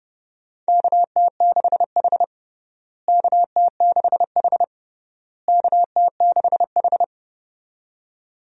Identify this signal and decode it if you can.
Morse code, heard as a keyed tone, about 30 wpm, 700 Hz. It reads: KT6H KT6H KT6H
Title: USA Call Signs - 4 Characters - Repeated 3x - 30wpm